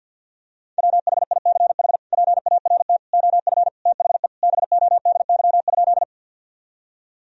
Morse code, heard as a keyed tone, about 48 wpm, 700 Hz. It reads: WHICH PART OF THE BODY?
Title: Sentences from Top 1000 Words - Review Part 2 - 50wpm